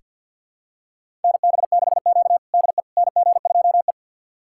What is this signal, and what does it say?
Morse code, heard as a keyed tone, about 46 wpm, 700 Hz. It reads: NB6Q DE NG2E
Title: USA Callsign DE Callsign - 4 Characters - 50wpm